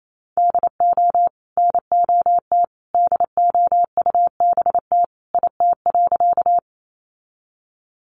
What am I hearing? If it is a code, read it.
Morse code, heard as a keyed tone, about 28 wpm, 700 Hz. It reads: DO NOT DOUBT IT.
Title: Sentences from Top 1500 Words 28wpm